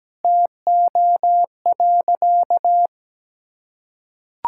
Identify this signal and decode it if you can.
Morse code, heard as a keyed tone, about 17 wpm, 700 Hz. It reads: TO.
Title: Sentences from Top 1600 Words 17wpm